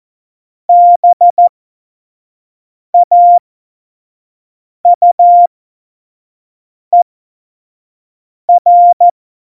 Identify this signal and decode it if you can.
Morse code, heard as a keyed tone, about 14 wpm, spaced as for 5 wpm, 700 Hz. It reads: BAUER
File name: Morsekurs / Witze / Witz 27. Oktober 2020, 25 F70